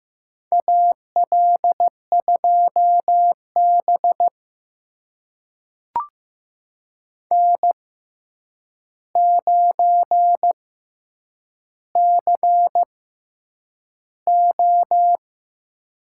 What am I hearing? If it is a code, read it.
Morse code, heard as a keyed tone, about 15 wpm, spaced as for 8 wpm, 700 Hz. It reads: AL2B N9CO
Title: USA Call Signs - 4 Characters - 6x Character Spacing - 15wpm